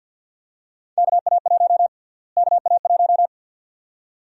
Morse code, heard as a keyed tone, about 47 wpm, 700 Hz. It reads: KA1 KA1
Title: USA Call Sign Prefixes - Repeated 2x - 50wpm